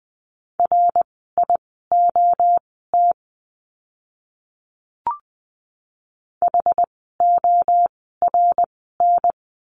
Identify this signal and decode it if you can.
Morse code, heard as a keyed tone, about 20 wpm, spaced as for 14 wpm, 700 Hz. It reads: RIOT HORN
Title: 061 Lesson 11f - Letter h - Words 20wpm